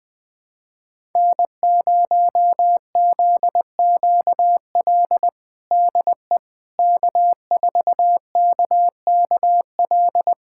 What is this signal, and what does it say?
Morse code, heard as a keyed tone, about 21 wpm, 700 Hz. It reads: N0ZQL DE K4KKL
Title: USA Callsign DE Callsign - 5 Characters - 20wpm